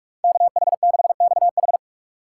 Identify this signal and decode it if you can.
Morse code, heard as a keyed tone, about 42 wpm, 700 Hz. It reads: KH6XH